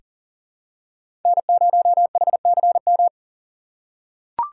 Morse code, heard as a keyed tone, about 37 wpm, 700 Hz. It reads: N0HCM
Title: USA Call Signs - 5 Characters - 2x Character Spacing - 40wpm